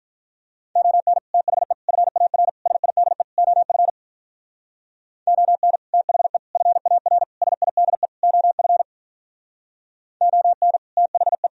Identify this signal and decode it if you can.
Morse code, heard as a keyed tone, about 44 wpm, 700 Hz. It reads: ON THE FAR SIDE OF ON THE FAR SIDE OF ON THE
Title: Speed-Racing - Sets of 5 Words 35wpm